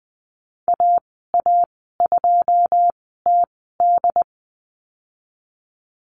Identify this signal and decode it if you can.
Morse code, heard as a keyed tone, about 20 wpm, spaced as for 15 wpm, 700 Hz. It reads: AA2TD
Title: 088 Lesson 15g - Number 5 - Call Signs 20wpm